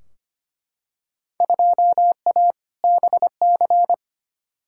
Morse code, heard as a keyed tone, about 25 wpm, 700 Hz. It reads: 2A BC
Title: Field Day Exchanges - 25wpm